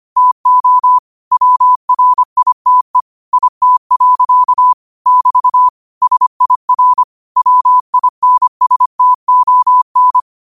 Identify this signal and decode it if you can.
Morse code, heard as a keyed tone, about 25 wpm, 1000 Hz. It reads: TO WRITE IT. <BT> SIR WINSTON